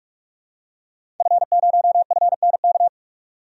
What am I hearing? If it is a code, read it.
Morse code, heard as a keyed tone, about 45 wpm, 700 Hz. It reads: F0FNK
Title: DXCC-100 Call Signs - 5 Characters - 45wpm